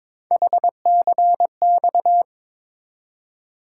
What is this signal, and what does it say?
Morse code, heard as a keyed tone, about 26 wpm, 700 Hz. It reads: HCX
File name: Speed-Racing - Three Letter-Number Combinations - 22wpm